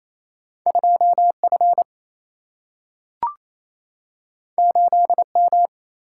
Morse code, heard as a keyed tone, about 28 wpm, 700 Hz. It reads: 2 F 8 M
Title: Single-Letter Single-Number 28wpm